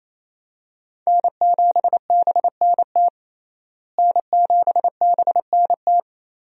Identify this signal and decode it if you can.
Morse code, heard as a keyed tone, about 28 wpm, 700 Hz. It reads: N7BNT N7BNT
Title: USA Call Signs - 5 Characters - Repeated 2x - 28wpm